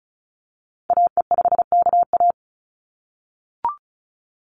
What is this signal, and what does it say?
Morse code, heard as a keyed tone, about 34 wpm, 700 Hz. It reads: AE5KA
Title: NA SOTA Activator Callsigns 35wpm